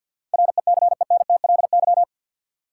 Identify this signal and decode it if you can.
Morse code, heard as a keyed tone, about 52 wpm, 700 Hz. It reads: RECENTLY
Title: Sentences from Top 1200 Words - New Word Drill - 50wpm